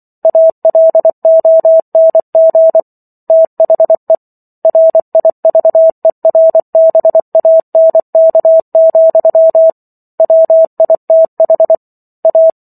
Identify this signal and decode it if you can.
Morse code, heard as a keyed tone, about 24 wpm, 650 Hz. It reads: ALONG THE RIVERBANK, WITH A